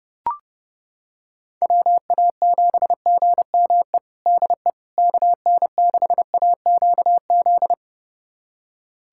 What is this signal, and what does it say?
Morse code, heard as a keyed tone, about 30 wpm, 700 Hz. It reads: WA7GME DE KN6AQZ
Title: USA Callsign DE Callsign 30wpm